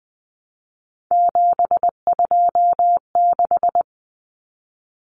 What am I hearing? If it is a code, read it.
Morse code, heard as a keyed tone, about 20 wpm, 700 Hz. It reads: 726
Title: Numbers - 3-Digit 20wpm